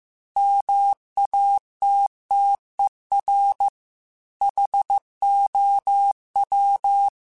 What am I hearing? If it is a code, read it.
Morse code, heard as a keyed tone, about 15 wpm, 800 Hz. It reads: MATTER HOW
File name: Technical Subjects / Technical Subjects-mp3s / AND Gate @13